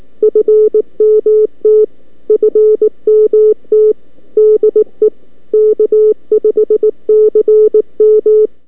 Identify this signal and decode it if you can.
Morse code, heard as a keyed tone, about 19 wpm, 430 Hz. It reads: FMT FMT DE K5CM
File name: k5cm 20m by abi-may9-2012